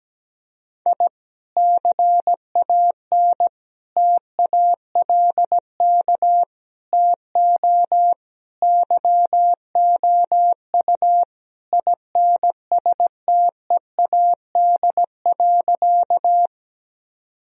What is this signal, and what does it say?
Morse code, heard as a keyed tone, about 17 wpm, 700 Hz. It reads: I CAN TALK TO YOU INSTEAD.
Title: Sentences from Top 1000 Words - Review Part 4 - 17wpm